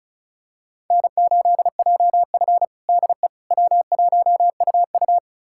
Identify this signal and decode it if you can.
Morse code, heard as a keyed tone, about 36 wpm, 700 Hz. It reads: N8JF DE W1UU
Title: USA Callsign DE Callsign - 4 Characters - 35wpm